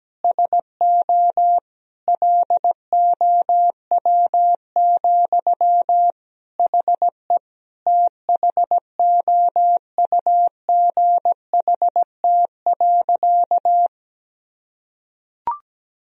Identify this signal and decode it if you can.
Morse code, heard as a keyed tone, about 17 wpm, 700 Hz. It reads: SO LOW, HE THOUGHT.
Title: Sentences from Top 1000 Words - Review Part 1 - 17wpm